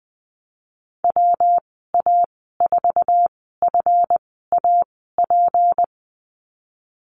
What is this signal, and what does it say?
Morse code, heard as a keyed tone, about 19 wpm, spaced as for 14 wpm, 700 Hz. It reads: WA4FAP
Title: 154 Lesson 25g - Letter p - Call Signs 20wpm